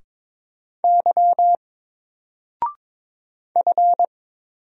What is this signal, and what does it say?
Morse code, heard as a keyed tone, about 22 wpm, 700 Hz. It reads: Y F
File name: Single-Letter-Number-22wpm